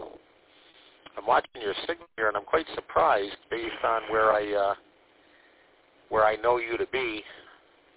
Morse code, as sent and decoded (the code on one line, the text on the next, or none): none